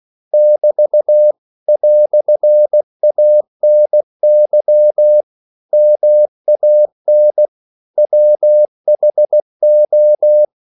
Morse code, heard as a keyed tone, about 16 wpm, 600 Hz. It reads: <BT> "ANY MAN WHO
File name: CW / Practice / EnglishQuotations0216